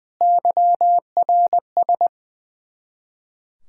Morse code, heard as a keyed tone, about 20 wpm, 700 Hz. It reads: YRS
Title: Practice QSO - Exchange 3 - More Info - 20wpm